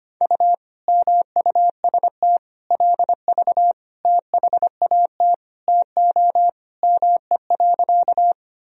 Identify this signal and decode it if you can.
Morse code, heard as a keyed tone, about 25 wpm, 700 Hz. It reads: U MUST LV THAT TO ME.
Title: Sentences from Top 400 Words and Abbreviations 25wpm